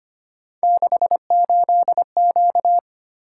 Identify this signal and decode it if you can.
Morse code, heard as a keyed tone, about 25 wpm, 700 Hz. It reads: 68Q
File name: Three Letter-Number Combinations - 25wpm